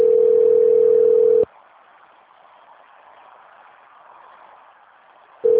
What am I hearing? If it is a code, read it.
Ringback tone (North America), 440 + 480 Hz.